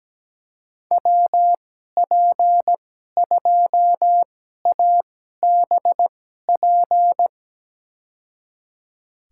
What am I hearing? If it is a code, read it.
Morse code, heard as a keyed tone, about 17 wpm, spaced as for 12 wpm, 700 Hz. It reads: WP2ABP
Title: USA Call Signs - 6 Characters - 2x Character Spacing - 17wpm